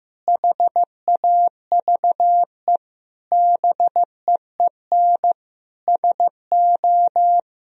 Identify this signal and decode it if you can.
Morse code, heard as a keyed tone, about 15 wpm, 700 Hz. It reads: HAVE BEEN SO
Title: Sentences from Top 1000 Words - Review Part 1 - 15wpm